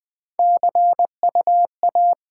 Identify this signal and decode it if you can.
Morse code, heard as a keyed tone, about 20 wpm, 700 Hz. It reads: C U A
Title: Top 20 QSO Elements - 8x Character Spacing - 20wpm